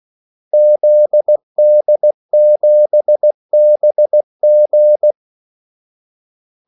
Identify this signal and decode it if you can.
Morse code, heard as a keyed tone, about 16 wpm, 600 Hz. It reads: ZD7BG
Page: Kaldesignaler 16 wpm